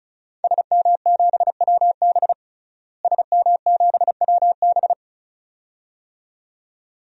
Morse code, heard as a keyed tone, about 34 wpm, 700 Hz. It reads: SM7WB SM7WB
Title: DXCC-100 Call Signs - 5 Characters - Repeated 2x - 35wpm